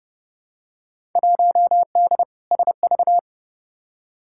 Morse code, heard as a keyed tone, about 30 wpm, 700 Hz. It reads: 1D SV
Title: Field Day Exchanges - 30wpm